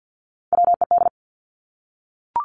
Morse code, heard as a keyed tone, about 50 wpm, 700 Hz. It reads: WED
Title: Three Challenging Letter-Number Combinations 50wpm